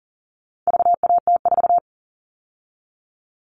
Morse code, heard as a keyed tone, about 39 wpm, 700 Hz. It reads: VAT4
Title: All Cut Numbers - 4-Digit 40wpm